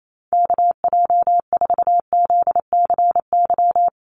Morse code, heard as a keyed tone, about 28 wpm, 700 Hz. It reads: KJ4ZCY
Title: USA Call Signs Repeated 2x 28wpm